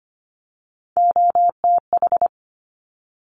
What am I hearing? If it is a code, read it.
Morse code, heard as a keyed tone, about 25 wpm, 700 Hz. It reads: OTH
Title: Top Three Letter Combinations in Words 25wpm